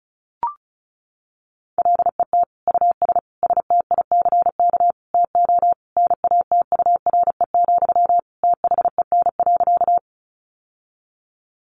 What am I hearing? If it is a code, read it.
Morse code, heard as a keyed tone, about 34 wpm, 700 Hz. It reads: LET US STICK TO NATURE, THEN.
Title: Sentences from Top 1200 Words 35wpm